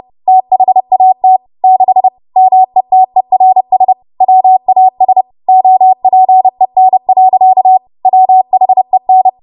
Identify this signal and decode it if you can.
Morse code, heard as a keyed tone, about 30 wpm, 750 Hz. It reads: THAT 6 METERS WAS OPEN. WHEN
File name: Morse / 30WPM